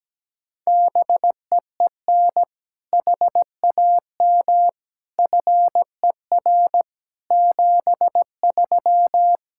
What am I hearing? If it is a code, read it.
Morse code, heard as a keyed tone, about 17 wpm, 700 Hz. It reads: BEEN HAM FER 73
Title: Practice QSO - Exchange 3 - More Info - 17wpm